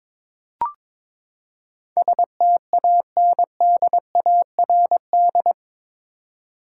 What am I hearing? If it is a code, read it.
Morse code, heard as a keyed tone, about 22 wpm, 700 Hz. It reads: STANDARD
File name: practice / 22wpm / Eight Letter Words - 22wpm